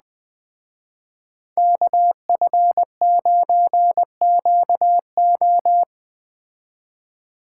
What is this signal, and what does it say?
Morse code, heard as a keyed tone, about 19 wpm, 700 Hz. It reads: KF9QO